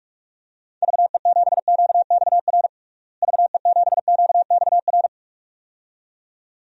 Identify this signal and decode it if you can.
Morse code, heard as a keyed tone, about 46 wpm, 700 Hz. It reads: VE7QXR VE7QXR
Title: DXCC-100 Call Signs - 6 Characters - Repeated 2x - 45wpm